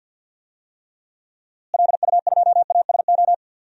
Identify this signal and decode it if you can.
Morse code, heard as a keyed tone, about 52 wpm, 700 Hz. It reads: LU2ASO
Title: DXCC-100 Call Signs - 6 Characters - Repeated 3x - 50wpm